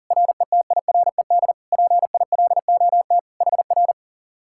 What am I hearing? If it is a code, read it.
Morse code, heard as a keyed tone, about 48 wpm, 700 Hz. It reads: RETIRED PILOT HR
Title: Speed-Racing Common QSO Phrases - 40wpm